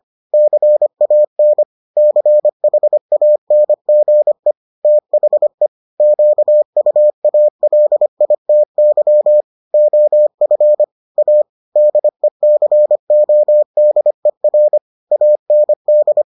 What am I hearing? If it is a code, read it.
Morse code, heard as a keyed tone, about 25 wpm, 600 Hz. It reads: CAN CHANGE THE QUALITY OF A DECODER AND